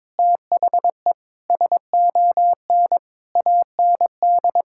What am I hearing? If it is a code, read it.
Morse code, heard as a keyed tone, about 22 wpm, 700 Hz. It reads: THE SON AND
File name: Sentences with 9-Word Phrases - 22wpm